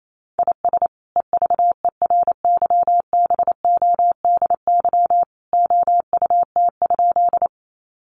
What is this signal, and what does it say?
Morse code, heard as a keyed tone, about 28 wpm, 700 Hz. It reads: IS EVERYBODY OUT?
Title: Sentences from Top 1400 Words 28wpm